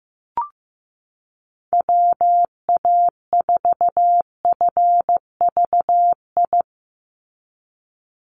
Morse code, heard as a keyed tone, about 15 wpm, 700 Hz. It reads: WA4FVI
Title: USA Call Signs 15wpm